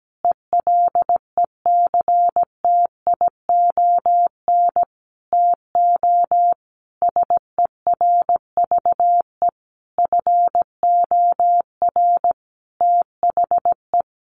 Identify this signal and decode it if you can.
Morse code, heard as a keyed tone, about 17 wpm, 700 Hz. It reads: ELECTION TO SERVE FOR THE
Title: Sentences from Top 1500 Words 17wpm